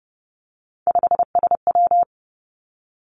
Morse code, heard as a keyed tone, about 30 wpm, 700 Hz. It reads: 5SW
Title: Three Challenging Letter-Number Combinations 30wpm